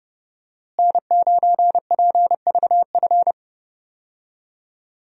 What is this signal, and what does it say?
Morse code, heard as a keyed tone, about 31 wpm, 700 Hz. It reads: N9PVF
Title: USA Call Signs - 5 Characters - Repeated 2x - 30wpm